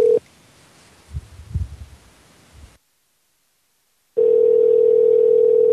Ringback tone (North America), 440 + 480 Hz.